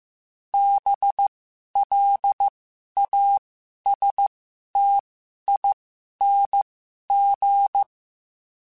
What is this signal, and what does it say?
Morse code, heard as a keyed tone, about 15 wpm, spaced as for 11 wpm, 800 Hz. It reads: BLASTING
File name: Special INGs-mp3s / Special INGs-mp3s / Sngl Ing @10